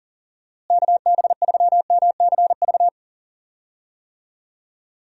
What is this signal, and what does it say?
Morse code, heard as a keyed tone, about 40 wpm, 700 Hz. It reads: KB3MCV
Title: USA Call Signs - 6 Characters - Repeated 2x - 40wpm